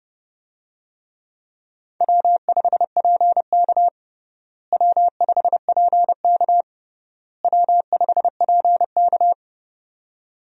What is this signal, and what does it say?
Morse code, heard as a keyed tone, about 30 wpm, 700 Hz. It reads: W5PK W5PK W5PK
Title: USA Call Signs - 4 Characters - Repeated 3x - 30wpm